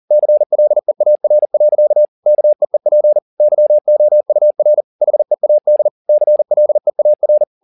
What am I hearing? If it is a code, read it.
Morse code, heard as a keyed tone, about 40 wpm, 600 Hz. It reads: CLEAR. KEEP YOUR HEAD CLEAR
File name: the old man and the sea 2 40wpm